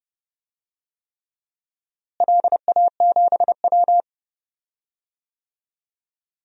Morse code, heard as a keyed tone, about 30 wpm, 700 Hz. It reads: LA7W
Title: DXCC-100 Call Signs - 4 Characters - Repeated 3x - 30wpm